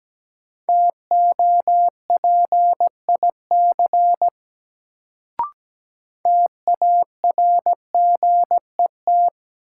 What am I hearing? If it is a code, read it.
Morse code, heard as a keyed tone, about 17 wpm, 700 Hz. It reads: TOPIC TARGET
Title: Sentences from Top 1200 Words - New Word Drill Alt - 17wpm